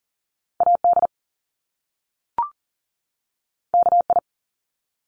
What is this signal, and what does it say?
Morse code, heard as a keyed tone, about 40 wpm, 700 Hz. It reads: AD KI
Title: Top Two Letter Combinations in Words 40wpm